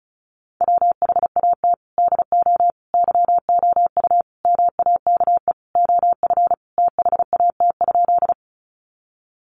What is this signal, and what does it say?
Morse code, heard as a keyed tone, about 34 wpm, 700 Hz. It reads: WHAT DO YOU MAKE OF THAT?
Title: Sentences from Top 100 Words 35wpm